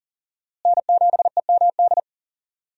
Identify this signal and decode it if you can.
Morse code, heard as a keyed tone, about 40 wpm, 700 Hz. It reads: N7EMD
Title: USA Call Signs - 5 Characters - Repeated 3x - 40wpm